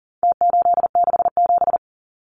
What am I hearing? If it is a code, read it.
Morse code, heard as a keyed tone, about 40 wpm, 700 Hz. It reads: T867
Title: Numbers - Contest Style - 4-Digit 40wpm